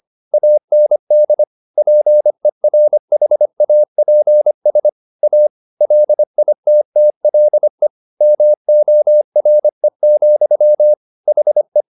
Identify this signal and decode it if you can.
Morse code, heard as a keyed tone, about 25 wpm, 600 Hz. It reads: AND PERHAPS A LITTLE MORE, HE
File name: the old man and the sea part 1 25wpm